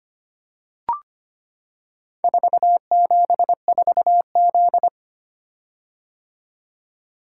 Morse code, heard as a keyed tone, about 25 wpm, 700 Hz. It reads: 474Z
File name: practice / 25wpm / Four Letter-Number Combinations - 25wpm